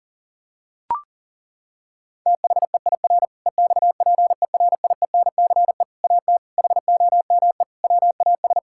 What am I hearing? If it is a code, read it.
Morse code, heard as a keyed tone, about 40 wpm, 700 Hz. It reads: THEIR EXPERIENCE AT HOME WAS
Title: Sentences from Top 1000 Words - Review Part 6 - 40wpm